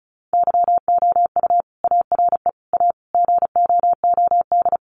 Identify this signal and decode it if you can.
Morse code, heard as a keyed tone, about 34 wpm, 700 Hz. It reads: YOU ARE A GOOD
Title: Sentences from Top 400 Words 35wpm